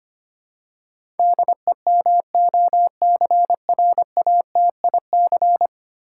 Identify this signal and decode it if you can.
Morse code, heard as a keyed tone, about 25 wpm, 700 Hz. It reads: DEMOCRATIC
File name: 10 Letter Words - First 3 Letters Given - 25wpm